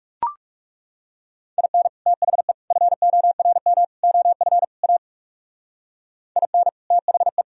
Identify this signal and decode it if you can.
Morse code, heard as a keyed tone, about 38 wpm, 700 Hz. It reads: IN THE FORM OF A IN THE